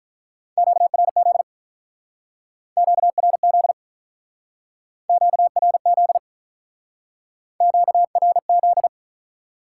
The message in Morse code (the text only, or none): QRZ QRZ QRZ QRZ